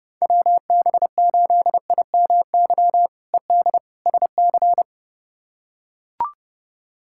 Morse code, heard as a keyed tone, about 30 wpm, 700 Hz. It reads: WB8IMY ED SC
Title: K1USN SST QSO Practice - 30wpm